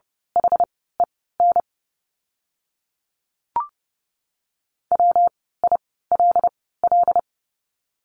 Morse code, heard as a keyed tone, about 30 wpm, spaced as for 17 wpm, 700 Hz. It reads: HEN WILL
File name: course / 30:10wpm / 115 Lesson 19f - Letter w - Words 30wpm